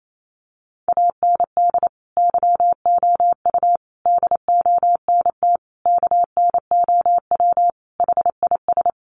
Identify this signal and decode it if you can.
Morse code, heard as a keyed tone, about 28 wpm, 700 Hz. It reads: AND YOU DONT KNOW HIS
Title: Sentences from Top 1000 Words 28wpm